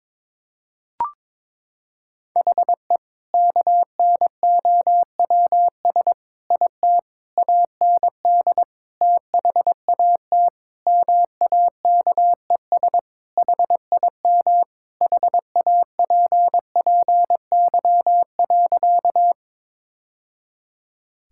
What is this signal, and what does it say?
Morse code, heard as a keyed tone, about 22 wpm, 700 Hz. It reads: HE KNOWS IT AND THAT MAKES HIM HAPPY.
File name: practice / 22wpm / Sentences with 2-Word Phrases - 22wpm